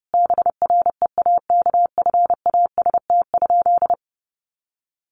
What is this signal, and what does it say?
Morse code, heard as a keyed tone, about 30 wpm, 700 Hz. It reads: BREAKFAST?
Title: Sentences from Top 1500 Words 30wpm